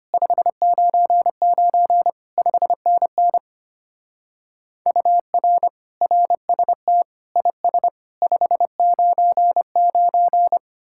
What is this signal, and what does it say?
Morse code, heard as a keyed tone, about 30 wpm, 700 Hz. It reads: 599 5NN UR RST IS 599
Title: Speed-Racing Common QSO Phrases - 25wpm